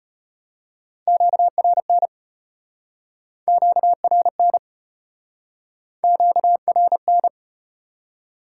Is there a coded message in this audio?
Yes: Morse code from 1.1 to 7.3 s, heard as a keyed tone, about 35 wpm, 700 Hz. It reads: QRN QRN QRN